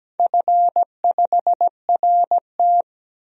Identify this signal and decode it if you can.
Morse code, heard as a keyed tone, about 17 wpm, 700 Hz. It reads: F5RT
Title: DXCC-100 Call Signs - 4 Characters - 17wpm